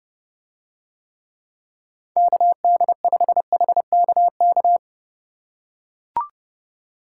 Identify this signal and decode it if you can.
Morse code, heard as a keyed tone, about 31 wpm, 700 Hz. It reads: KD5HKK